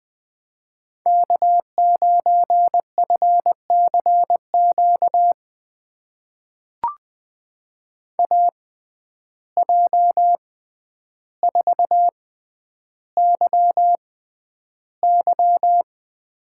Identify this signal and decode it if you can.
Morse code, heard as a keyed tone, about 20 wpm, spaced as for 10 wpm, 700 Hz. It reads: K9FCQ AJ4YY
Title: USA Call Signs - 5 Characters - 6x Character Spacing - 20wpm